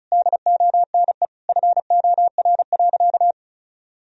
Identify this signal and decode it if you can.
Morse code, heard as a keyed tone, about 34 wpm, 700 Hz. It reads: DONE FOR.
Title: Sentences from Top 1000 Words - Review Part 4 - 35wpm